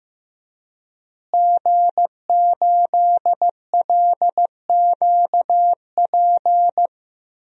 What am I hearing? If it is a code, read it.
Morse code, heard as a keyed tone, about 15 wpm, 700 Hz. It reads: G8LQP
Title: DXCC-100 Call Signs - 5 Characters - Repeated 2x - 15wpm